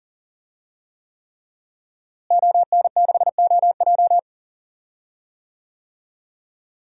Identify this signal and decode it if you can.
Morse code, heard as a keyed tone, about 40 wpm, 700 Hz. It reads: ON6OJ
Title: DXCC-100 Call Signs - 5 Characters - Repeated 3x - 40wpm